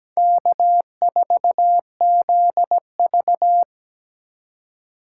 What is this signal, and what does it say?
Morse code, heard as a keyed tone, about 20 wpm, 700 Hz. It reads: K4ZV
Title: Speed Racing - USA 1x2 Call Signs 17wpm